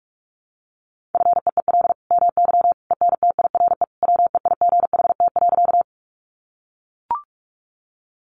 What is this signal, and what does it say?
Morse code, heard as a keyed tone, about 44 wpm, 700 Hz. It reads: FEEL MY ENTIRE WEIGHT.
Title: Sentences from Top 800 Words 45wpm